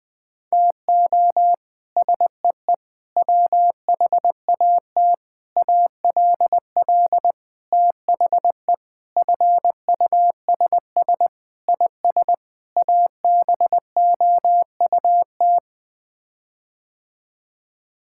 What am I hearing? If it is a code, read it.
Morse code, heard as a keyed tone, about 20 wpm, 700 Hz. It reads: TO SEE WHAT ALL THE FUSS IS ABOUT
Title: Sets of 8 Words - 20wpm